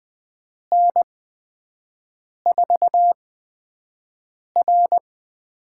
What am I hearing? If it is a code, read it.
Morse code, heard as a keyed tone, about 20 wpm, spaced as for 5 wpm, 700 Hz. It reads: N4R
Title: USA Call Signs - 5 Characters - 8x Character Spacing - 20wpm